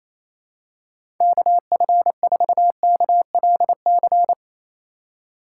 Morse code, heard as a keyed tone, about 28 wpm, 700 Hz. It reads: KF4KLC
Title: USA Call Signs - 6 Characters - 4x Character Spacing - 28wpm